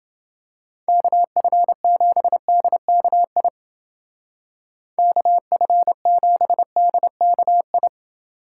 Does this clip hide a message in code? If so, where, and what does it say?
Yes: Morse code from 0.9 to 7.9 s, heard as a keyed tone, about 28 wpm, 700 Hz. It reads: KF7DKI KF7DKI